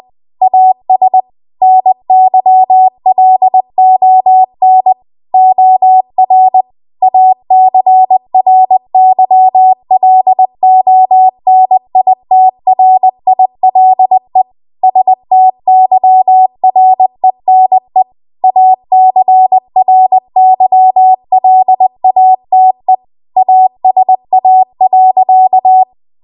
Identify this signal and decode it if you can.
Morse code, heard as a keyed tone, about 20 wpm, 750 Hz. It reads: AS NYLON OR ACRYLONITRILE STYRENE ACRYLATE ASA.
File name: Archive / 20 WPM / 20WPM